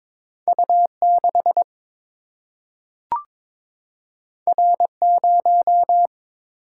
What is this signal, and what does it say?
Morse code, heard as a keyed tone, about 22 wpm, 700 Hz. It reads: U 6 R 0
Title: Single-Letter Single-Number 22wpm